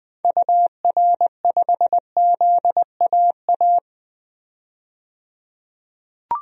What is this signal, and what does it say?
Morse code, heard as a keyed tone, about 20 wpm, 700 Hz. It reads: UR5ZAA
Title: DXCC-100 Call Signs - 6 Characters - 20wpm